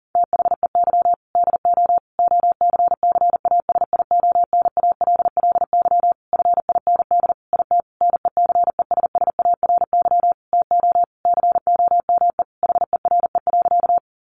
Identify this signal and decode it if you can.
Morse code, heard as a keyed tone, about 40 wpm, 700 Hz. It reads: THEY DO OCCASIONALLY FIND IT NECESSARY TO COME HERE.
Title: Sentences from Top 1300 Words 40wpm